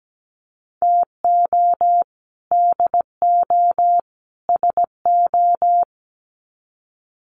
Morse code, heard as a keyed tone, about 17 wpm, 700 Hz. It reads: TO DO SO
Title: Sets of 3 Words 17wpm